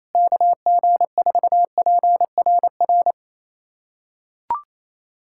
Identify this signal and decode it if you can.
Morse code, heard as a keyed tone, about 28 wpm, 700 Hz. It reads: KG4PRR
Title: USA Callsign DE Callsign - 6 Characters - 28wpm